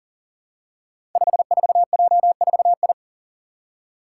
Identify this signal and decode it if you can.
Morse code, heard as a keyed tone, about 42 wpm, 700 Hz. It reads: 54J4I
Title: Five Challenging Letter-Number Combinations 40wpm